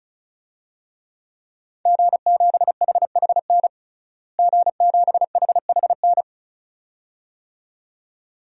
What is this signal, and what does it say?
Morse code, heard as a keyed tone, about 34 wpm, 700 Hz. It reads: G7HHN G7HHN
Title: DXCC-100 Call Signs - 5 Characters - Repeated 2x - 35wpm